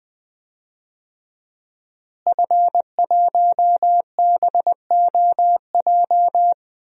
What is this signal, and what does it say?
Morse code, heard as a keyed tone, about 21 wpm, 700 Hz. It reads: F1BOJ
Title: DXCC-100 Call Signs - 5 Characters - Repeated 3x - 20wpm